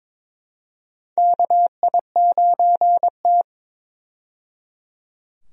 Morse code, heard as a keyed tone, about 22 wpm, 700 Hz. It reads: KI9T
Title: USA Call Signs - 4 Characters - 22wpm